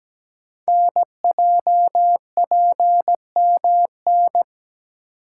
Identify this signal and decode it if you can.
Morse code, heard as a keyed tone, about 17 wpm, 700 Hz. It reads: NJPMN
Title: Code Groups - 5 Letters - 17wpm